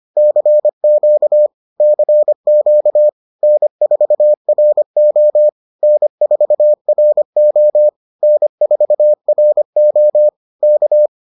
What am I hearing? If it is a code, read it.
Morse code, heard as a keyed tone, about 25 wpm, 600 Hz. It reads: CQ CQ N4RO N4RO N4RO K